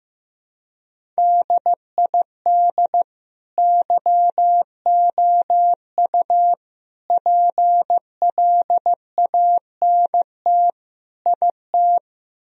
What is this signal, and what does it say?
Morse code, heard as a keyed tone, about 15 wpm, 700 Hz. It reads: DID YOU PLANT IT
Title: Sentences from Top 1000 Words - Review Part 1 - 15wpm